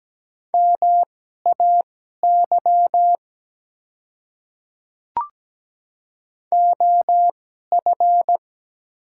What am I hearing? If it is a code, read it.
Morse code, heard as a keyed tone, about 17 wpm, spaced as for 12 wpm, 700 Hz. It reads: MAY OF
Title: 2x Character Spacing - Top 100 Words - 17wpm